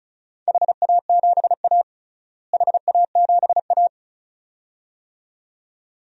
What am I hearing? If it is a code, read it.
Morse code, heard as a keyed tone, about 36 wpm, 700 Hz. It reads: HA7A HA7A
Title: DXCC-100 Call Signs - 4 Characters - Repeated 2x - 35wpm